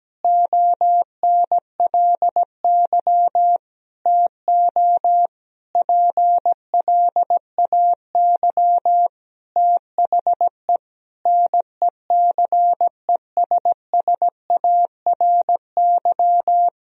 Morse code, heard as a keyed tone, about 17 wpm, 700 Hz. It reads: ONLY TO PLAY THE NECESSARY
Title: Sentences from Top 1000 Words - Review Part 1 - 17wpm